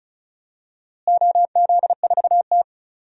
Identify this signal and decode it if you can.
Morse code, heard as a keyed tone, about 34 wpm, 700 Hz. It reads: OZ4T
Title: DXCC-100 Call Signs - 4 Characters - Repeated 2x - 35wpm